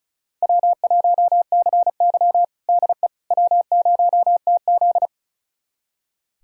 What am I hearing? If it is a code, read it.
Morse code, heard as a keyed tone, about 36 wpm, 700 Hz. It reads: W1CY DE W0TZ
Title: USA Callsign DE Callsign - 4 Characters - 35wpm